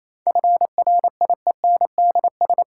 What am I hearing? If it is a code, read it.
Morse code, heard as a keyed tone, about 28 wpm, 700 Hz. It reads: FRIENDS
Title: Sentences from Top 1000 Words - Review Part 5 - New Word Drill - 28wpm